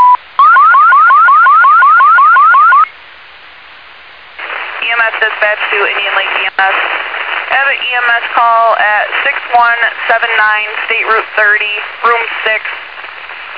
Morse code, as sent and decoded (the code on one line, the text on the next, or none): none